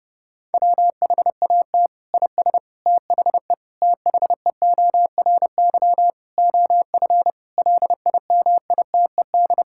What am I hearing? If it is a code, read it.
Morse code, heard as a keyed tone, about 32 wpm, 700 Hz. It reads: WHAT IS THE THEORY OF LIMITED
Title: Sentences from Top 1000 Words - Review Part 6 - 30wpm